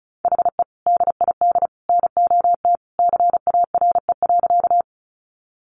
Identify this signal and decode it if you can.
Morse code, heard as a keyed tone, about 34 wpm, 700 Hz. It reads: HE DID NOT CARE.
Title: Sentences from Top 600 Words 35wpm